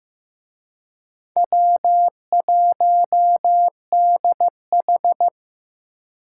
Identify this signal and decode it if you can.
Morse code, heard as a keyed tone, about 17 wpm, 700 Hz. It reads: W1DH
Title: Speed Racing - USA 1x2 Call Signs 15wpm